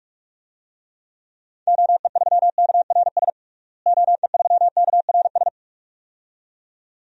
Morse code, heard as a keyed tone, about 42 wpm, 700 Hz. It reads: OE3KRS OE3KRS
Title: DXCC-100 Call Signs - 6 Characters - Repeated 2x - 45wpm